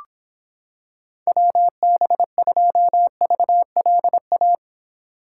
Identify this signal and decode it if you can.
Morse code, heard as a keyed tone, about 26 wpm, 700 Hz. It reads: WB2VLA